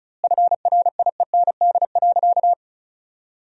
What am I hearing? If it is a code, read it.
Morse code, heard as a keyed tone, about 34 wpm, 700 Hz. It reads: FRIEND.
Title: 2x Word-Spacing - Sentences from Top 300 Words - 35wpm